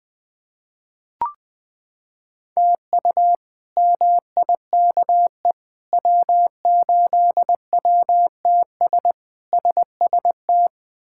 Morse code, heard as a keyed tone, about 20 wpm, 700 Hz. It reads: TU MIKE W8WTS SST
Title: K1USN SST QSO Practice - 20wpm